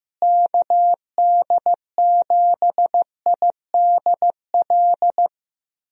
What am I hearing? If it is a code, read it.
Morse code, heard as a keyed tone, about 15 wpm, 700 Hz. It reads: KD7IDL
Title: USA Call Signs - 6 Characters - Repeated 2x - 15wpm